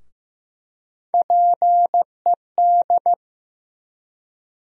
Morse code, heard as a keyed tone, about 15 wpm, 700 Hz. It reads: PED
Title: USA Call Sign Postfixes - 3 Letters 15wpm